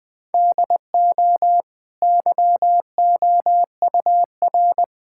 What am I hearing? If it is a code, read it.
Morse code, heard as a keyed tone, about 20 wpm, 700 Hz. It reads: DO YOUR
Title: Easy Sentences with Hints - 20wpm